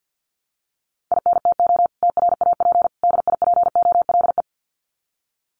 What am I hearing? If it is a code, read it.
Morse code, heard as a keyed tone, about 50 wpm, 700 Hz. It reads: INTO TRAP DIPOLE
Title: Common QSO Phrases 50wpm